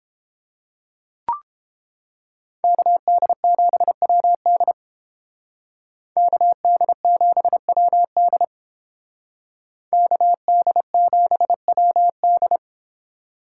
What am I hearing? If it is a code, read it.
Morse code, heard as a keyed tone, about 31 wpm, 700 Hz. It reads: KD7WD KD7WD KD7WD